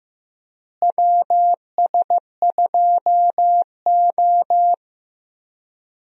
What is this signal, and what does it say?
Morse code, heard as a keyed tone, about 15 wpm, 700 Hz. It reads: WS2O
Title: USA Call Signs - 4 Characters - Repeated 3x - 15wpm